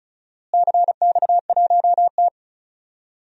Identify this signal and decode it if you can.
Morse code, heard as a keyed tone, about 36 wpm, 700 Hz. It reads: CX1T
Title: DXCC-100 Call Signs - 4 Characters - Repeated 2x - 35wpm